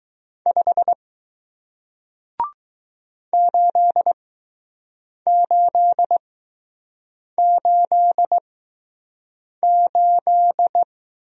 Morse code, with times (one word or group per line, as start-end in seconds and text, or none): 0.5-0.9 5
3.3-10.8 8888